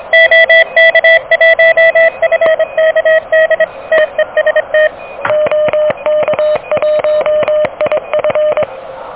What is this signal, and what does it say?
Morse code, heard as a keyed tone, about 28 wpm, 590 Hz. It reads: OK1FKD TEST OK1IF
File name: OK QRP 2023 / OK QRP 2023